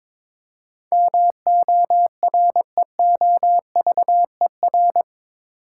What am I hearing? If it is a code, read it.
Morse code, heard as a keyed tone, about 22 wpm, 700 Hz. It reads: MOREOVER
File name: practice / 22wpm / Eight Letter Words - 22wpm